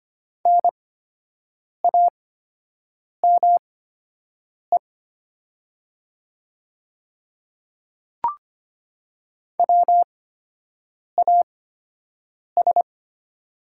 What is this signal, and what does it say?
Morse code, heard as a keyed tone, about 25 wpm, spaced as for 7 wpm, 700 Hz. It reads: NAME WAS
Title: 8x Character Spacing - Top 200 Words - 25wpm